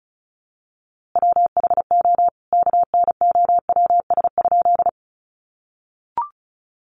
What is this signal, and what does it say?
Morse code, heard as a keyed tone, about 34 wpm, 700 Hz. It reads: WHO KNOWS?
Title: A Sound Of Thunder 35wpm